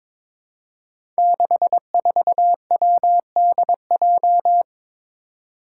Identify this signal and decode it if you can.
Morse code, heard as a keyed tone, about 22 wpm, 700 Hz. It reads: 64WDJ
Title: Five Challenging Letter-Number Combinations 22wpm